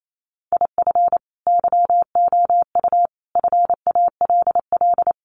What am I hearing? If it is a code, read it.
Morse code, heard as a keyed tone, about 28 wpm, 700 Hz. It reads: IF YOU FALL
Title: Sentences from Top 500 Words 28wpm